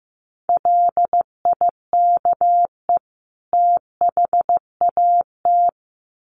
Morse code, heard as a keyed tone, about 15 wpm, 700 Hz. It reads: LIKE THAT
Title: Sets of 2 Words 15wpm